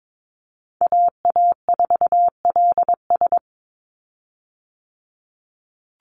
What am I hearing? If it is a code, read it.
Morse code, heard as a keyed tone, about 22 wpm, 700 Hz. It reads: AA4LS
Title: Call Signs Repeated 2x - 22wpm